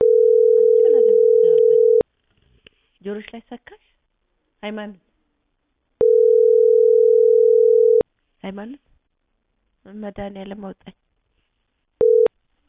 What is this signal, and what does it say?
Ringback tone (North America), 440 + 480 Hz, with speech.
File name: outgoing recordings